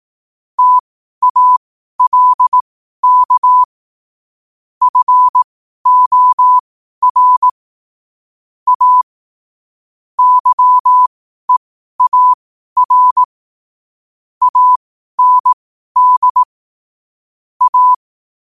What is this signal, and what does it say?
Morse code, heard as a keyed tone, about 18 wpm, spaced as for 12 wpm, 1000 Hz. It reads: TALK FOR A YEAR AND A
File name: qotdfiles / 12wpm